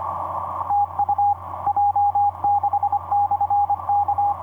Morse code, heard as a keyed tone, about 25 wpm, 820 Hz, with radio noise. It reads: TU J6/K
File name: j6-k8eab dl5zb etc 30m cw 06dec2010 2022